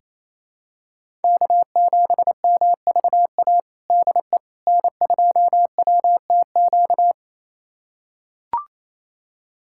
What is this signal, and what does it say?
Morse code, heard as a keyed tone, about 28 wpm, 700 Hz. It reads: K7MVA DE N2WTQ
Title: USA Callsign DE Callsign - 5 Characters - 28wpm